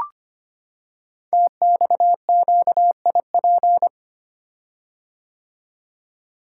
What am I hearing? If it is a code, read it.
Morse code, heard as a keyed tone, about 25 wpm, 700 Hz. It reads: TXQIP
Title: Code Groups - 5 Letters - 25wpm